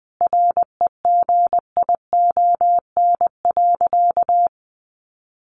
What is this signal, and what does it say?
Morse code, heard as a keyed tone, about 19 wpm, 700 Hz. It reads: REGION.
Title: Sentences from Top 1500 Words 20wpm